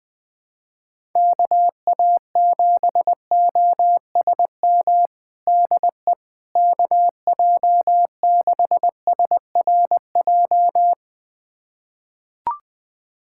Morse code, heard as a keyed tone, about 20 wpm, 700 Hz. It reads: KA7OSM DE KJ6SRJ
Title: USA Callsign DE Callsign 20wpm